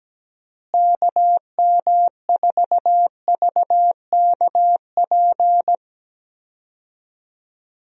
Morse code, heard as a keyed tone, about 17 wpm, 700 Hz. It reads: KM4VKP
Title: USA Call Signs - 6 Characters - Repeated 2x - 17wpm